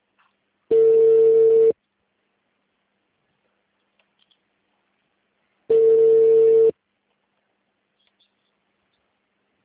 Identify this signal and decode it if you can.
Ringback tone (China, 450 Hz).
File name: out- -2043- - - .6048